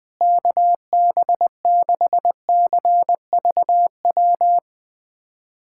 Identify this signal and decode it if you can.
Morse code, heard as a keyed tone, about 21 wpm, 700 Hz. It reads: KB6CVW
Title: USA Callsign DE Callsign 20wpm